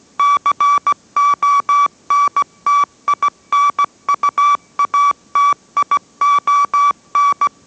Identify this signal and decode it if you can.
Morse code, heard as a keyed tone, about 17 wpm, 1150 Hz. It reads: CONTINUATION